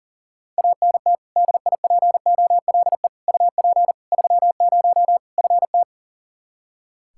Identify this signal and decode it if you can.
Morse code, heard as a keyed tone, about 40 wpm, 700 Hz. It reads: ANT DIPOLE UP 30 FT
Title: K1USN SST QSO Practice 40wpm